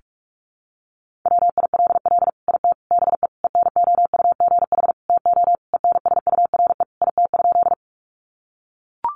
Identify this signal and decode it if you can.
Morse code, heard as a keyed tone, about 44 wpm, 700 Hz. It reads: WILL IT BE ENOUGH TO ENSURE IT?
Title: Sentences from Top 1100 Words 45wpm